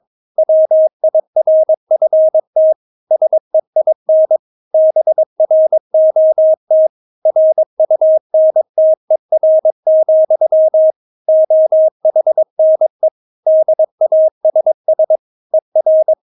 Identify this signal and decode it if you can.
Morse code, heard as a keyed tone, about 22 wpm, 630 Hz. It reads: WIRFT SEIN BROT RUNTER, OHNE DASS ER
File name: Witz3 22wpm